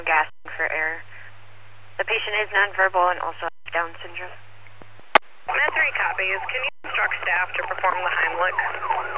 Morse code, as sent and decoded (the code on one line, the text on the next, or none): none